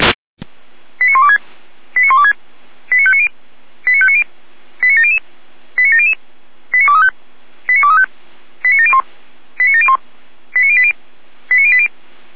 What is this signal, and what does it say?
Five-tone selective call (ZVEI), heard as quick runs of five tones, tones of about 70 ms. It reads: 89126 89126 89590 89590 89700 89700 89235 89235 89701 89701 89080 89080
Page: SDR/ZVEI